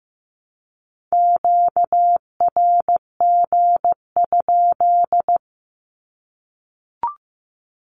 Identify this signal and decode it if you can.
Morse code, heard as a keyed tone, about 15 wpm, 700 Hz. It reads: QRG?
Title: Q-Codes 15wpm